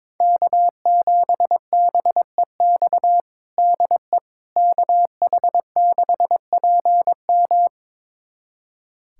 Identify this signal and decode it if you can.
Morse code, heard as a keyed tone, about 22 wpm, 700 Hz. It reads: K7BEX DE KH6PM
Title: USA Callsign DE Callsign - 5 Characters - 22wpm